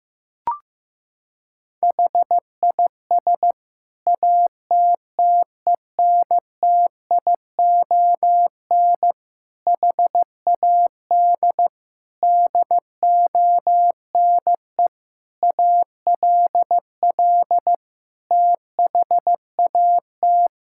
Morse code, heard as a keyed tone, about 15 wpm, 700 Hz. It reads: HIS ATTENTION HAD DONE ALL THAT
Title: Sentences from Top 1000 Words - Review Part 6 - 15wpm